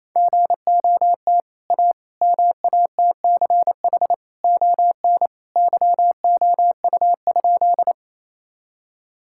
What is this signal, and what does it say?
Morse code, heard as a keyed tone, about 27 wpm, 700 Hz. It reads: GOT A MATCH ON YOU?
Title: Sentences from Top 1600 Words 28wpm